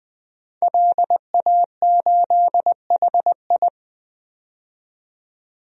Morse code, heard as a keyed tone, about 20 wpm, 700 Hz. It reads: LA8HI
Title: DXCC-100 Call Signs - 5 Characters - 20wpm